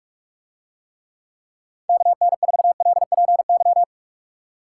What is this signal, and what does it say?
Morse code, heard as a keyed tone, about 46 wpm, 700 Hz. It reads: KN4LPY